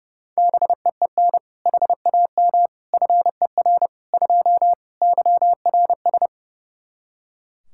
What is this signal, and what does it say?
Morse code, heard as a keyed tone, about 30 wpm, 700 Hz. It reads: BEEN HAM FER 2 YRS
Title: K1USN SST QSO Practice 30wpm